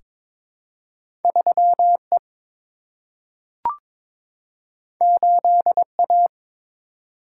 Morse code, heard as a keyed tone, about 21 wpm, 700 Hz. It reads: 3 E 8 A
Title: Single-Letter Single-Number 22wpm